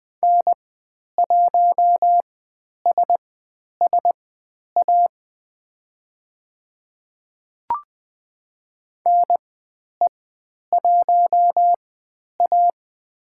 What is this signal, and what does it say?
Morse code, heard as a keyed tone, about 20 wpm, spaced as for 10 wpm, 700 Hz. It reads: N1SSA NE1A
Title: 095 Lesson 16g - Letter c - Call Signs 20wpm